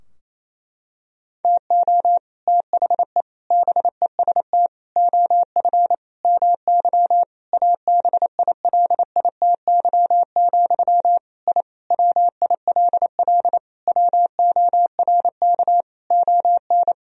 Morse code, heard as a keyed tone, about 28 wpm, 700 Hz. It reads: TO THE BEST OF MY ABILITY, I WILL WORK ON
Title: Sentences from Top 1000 Words - Review Part 3 - 28wpm